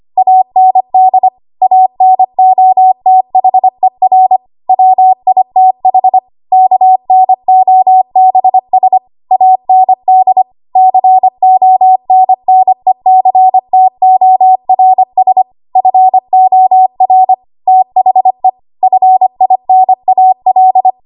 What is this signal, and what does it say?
Morse code, heard as a keyed tone, about 25 wpm, 750 Hz. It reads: AND ANOTHER WITH KNOBS AND CONNECTORS FOR THE FINAL
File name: Morse / 25WPM